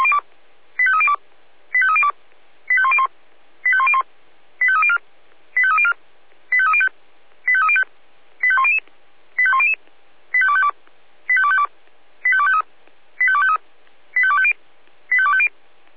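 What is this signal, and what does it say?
Five-tone selective call (ZVEI), heard as quick runs of five tones, tones of about 70 ms. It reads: 85302 85302 85101 85101 85304 85304 85305 85305 85100 85100 85252 85252 85253 85253 85250 85250
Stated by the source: Five-tone ZVEI selective call 85202 received via SDR.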